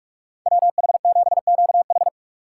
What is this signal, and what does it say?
Morse code, heard as a keyed tone, about 57 wpm, 700 Hz. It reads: WH7QH